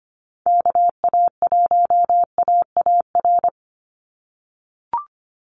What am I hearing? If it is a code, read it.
Morse code, heard as a keyed tone, about 25 wpm, 700 Hz. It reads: KA1AAR
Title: USA Call Signs 25wpm